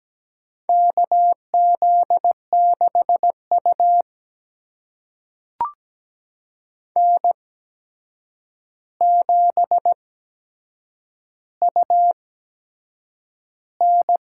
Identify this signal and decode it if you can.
Morse code, heard as a keyed tone, about 17 wpm, spaced as for 7 wpm, 700 Hz. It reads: KZ6U N7UN
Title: USA Call Signs - 4 Characters - 8x Character Spacing - 17wpm